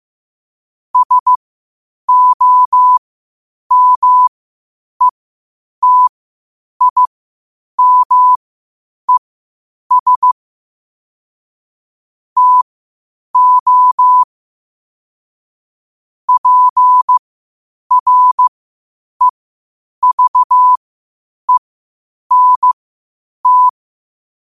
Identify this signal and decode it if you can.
Morse code, heard as a keyed tone, about 15 wpm, spaced as for 8 wpm, 1000 Hz. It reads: SOMETIMES TO PREVENT
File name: qotdfiles / 8wpm